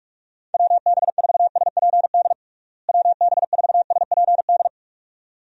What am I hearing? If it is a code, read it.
Morse code, heard as a keyed tone, about 46 wpm, 700 Hz. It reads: WB4SPD WB4SPD
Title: USA Call Signs - 6 Characters - Repeated 2x - 45wpm